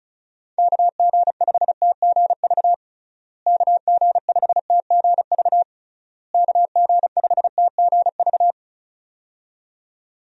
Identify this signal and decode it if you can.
Morse code, heard as a keyed tone, about 36 wpm, 700 Hz. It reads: KG5TGV KG5TGV KG5TGV
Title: USA Call Signs - 6 Characters - Repeated 3x - 35wpm